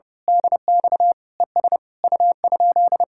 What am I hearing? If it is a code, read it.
Morse code, heard as a keyed tone, about 30 wpm, 700 Hz. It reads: DX ES U?
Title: Speed-Racing Common QSO Phrases - 30wpm